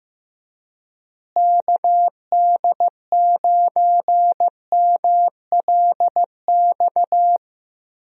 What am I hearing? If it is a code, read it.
Morse code, heard as a keyed tone, about 15 wpm, 700 Hz. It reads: KD9MLX
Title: USA Call Signs - 6 Characters - Repeated 3x - 15wpm